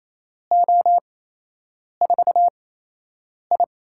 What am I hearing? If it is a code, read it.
Morse code, heard as a keyed tone, about 28 wpm, spaced as for 8 wpm, 700 Hz. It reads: O4I
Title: USA Call Signs - 4 Characters - 8x Character Spacing - 28wpm